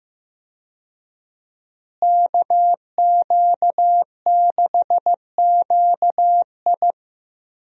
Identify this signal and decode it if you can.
Morse code, heard as a keyed tone, about 15 wpm, 700 Hz. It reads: KQ6QI
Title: USA Call Signs - 5 Characters - Repeated 2x - 15wpm